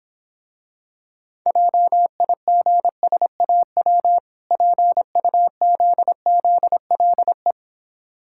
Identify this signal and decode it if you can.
Morse code, heard as a keyed tone, about 26 wpm, 700 Hz. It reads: JIGSAW PUZZLE